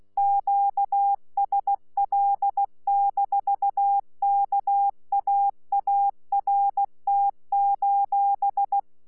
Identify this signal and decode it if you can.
Morse code, heard as a keyed tone, about 16 wpm, 800 Hz. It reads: QSL-KAART:
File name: Z PI4VRZ / PI4VRZ 14-jun-2025